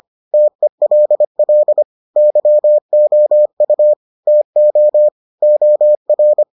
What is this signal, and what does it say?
Morse code, heard as a keyed tone, about 25 wpm, 600 Hz. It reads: TELL YOU TO OR